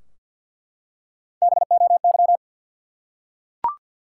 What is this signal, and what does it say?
Morse code, heard as a keyed tone, about 49 wpm, 700 Hz. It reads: BOY
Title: Three Letter Words - 50wpm